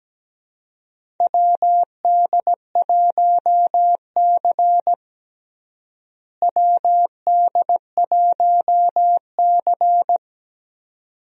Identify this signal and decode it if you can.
Morse code, heard as a keyed tone, about 17 wpm, 700 Hz. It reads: WD1C WD1C
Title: USA Call Signs - 4 Characters - Repeated 2x - 17wpm